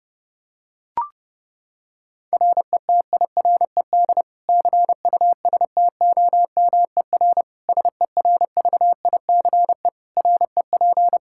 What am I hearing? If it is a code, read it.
Morse code, heard as a keyed tone, about 32 wpm, 700 Hz. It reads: RETIRED CUSTOMER SERVICE REP
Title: Practice QSO - Exchange 3 - More Info - 30wpm